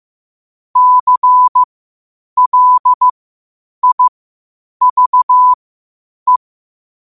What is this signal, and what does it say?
Morse code, heard as a keyed tone, about 15 wpm, spaced as for 8 wpm, 1000 Hz. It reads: CLIVE
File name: qotdfiles / 8wpm